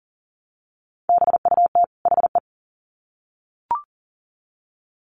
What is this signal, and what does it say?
Morse code, heard as a keyed tone, about 40 wpm, 700 Hz. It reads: BUT HE
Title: Sets of 2 Words 40wpm